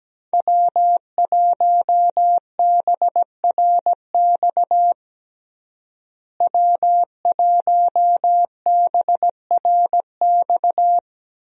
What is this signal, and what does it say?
Morse code, heard as a keyed tone, about 17 wpm, 700 Hz. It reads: W1BRX W1BRX
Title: USA Call Signs - 5 Characters - Repeated 3x - 17wpm